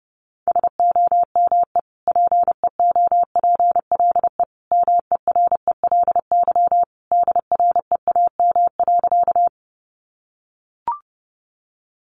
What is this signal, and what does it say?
Morse code, heard as a keyed tone, about 30 wpm, 700 Hz. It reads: SOME PEOPLE MERELY DREAM.
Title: Sentences from Top 1500 Words 30wpm